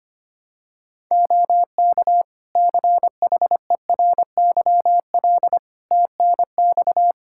Morse code, heard as a keyed tone, about 25 wpm, 700 Hz. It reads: OK CHERYL TNX
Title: Practice QSO - Exchange 4 - Ending - 25wpm